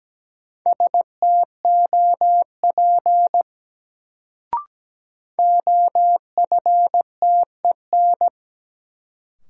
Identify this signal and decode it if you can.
Morse code, heard as a keyed tone, about 17 wpm, 700 Hz. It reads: STOP OFTEN
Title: Sentences from Top 300 Words - New Word Drill - 17wpm